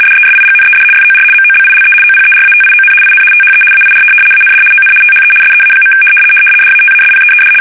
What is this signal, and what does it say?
RTTY, about 75 baud. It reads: DESIGNED FOR MOBILE OPERATION IN THE AIR AND ON THE SEA. DATASPEED
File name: fsk850-75B / teletype-corp